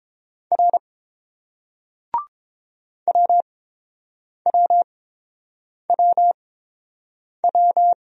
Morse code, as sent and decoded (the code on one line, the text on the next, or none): .-. / .-- .-- .-- .--
R WWWW